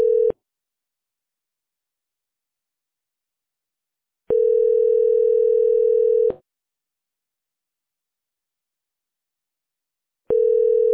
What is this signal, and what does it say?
Ringback tone (North America), 440 + 480 Hz.